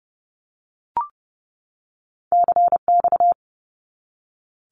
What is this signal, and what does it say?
Morse code, heard as a keyed tone, about 30 wpm, 700 Hz. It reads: CX
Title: Abbreviations 30wpm